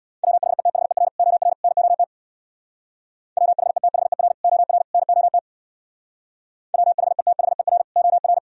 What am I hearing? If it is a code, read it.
Morse code, heard as a keyed tone, about 64 wpm, 700 Hz. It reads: WHETHER OR NOT WHETHER OR NOT WHETHER OR